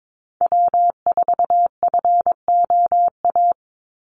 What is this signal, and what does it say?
Morse code, heard as a keyed tone, about 22 wpm, 700 Hz. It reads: W4FOA
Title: Call Signs - 22wpm